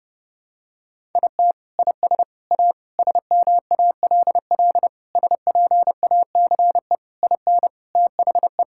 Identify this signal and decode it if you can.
Morse code, heard as a keyed tone, about 32 wpm, 700 Hz. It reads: IT IS A SMALL SPACE IN THE
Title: Sentences from Top 1000 Words - Review Part 5 - 30wpm